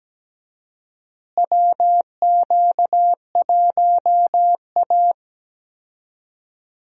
Morse code, heard as a keyed tone, about 17 wpm, 700 Hz. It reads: WQ1A
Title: USA Call Signs - 4 Characters - Repeated 2x - 17wpm